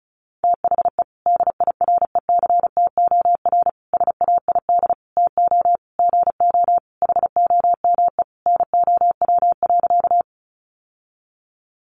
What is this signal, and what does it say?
Morse code, heard as a keyed tone, about 34 wpm, 700 Hz. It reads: THE DIRECTOR SAID TO GO HOME NOW.
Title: Sentences from Top 1300 Words 35wpm